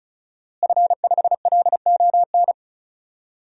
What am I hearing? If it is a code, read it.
Morse code, heard as a keyed tone, about 36 wpm, 700 Hz. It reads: F5LON
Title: DXCC-100 Call Signs - 5 Characters - Repeated 2x - 35wpm